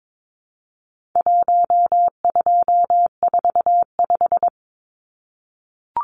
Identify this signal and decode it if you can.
Morse code, heard as a keyed tone, about 22 wpm, 700 Hz. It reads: 1245
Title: Numbers - Contest Style - 4-Digit 22wpm